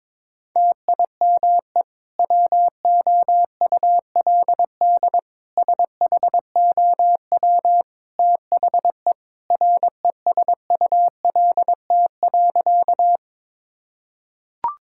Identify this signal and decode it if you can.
Morse code, heard as a keyed tone, about 22 wpm, 700 Hz. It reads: TIME WOULD SHOW THE RESULT.
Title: Sentences from Top 1000 Words - Review Part 5 - 22wpm